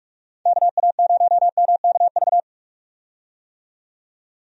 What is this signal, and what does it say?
Morse code, heard as a keyed tone, about 44 wpm, 700 Hz. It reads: KA0MKV
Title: USA Call Signs - 6 Characters - 45wpm